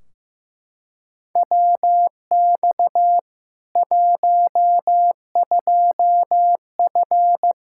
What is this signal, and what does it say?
Morse code, heard as a keyed tone, about 15 wpm, 700 Hz. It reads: WX 12F
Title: Practice QSO - Exchange 2 - Info - 15wpm